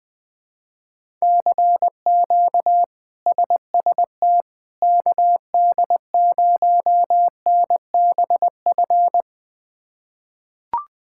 Morse code, heard as a keyed tone, about 21 wpm, 700 Hz. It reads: CQ SST KD0NBF
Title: K1USN SST QSO Practice - 20wpm